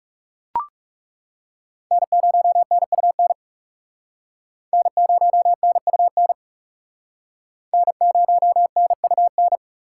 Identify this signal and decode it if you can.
Morse code, heard as a keyed tone, about 38 wpm, 700 Hz. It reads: N0NUN N0NUN N0NUN